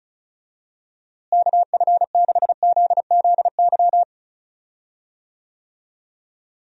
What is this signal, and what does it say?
Morse code, heard as a keyed tone, about 36 wpm, 700 Hz. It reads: KF6ZZY
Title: USA Call Signs - 6 Characters - Repeated 3x - 35wpm